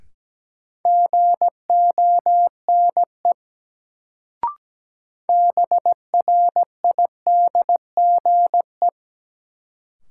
Morse code, heard as a keyed tone, about 17 wpm, 700 Hz. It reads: GONE BRIDGE
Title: Sentences from Top 1200 Words - New Word Drill - 17wpm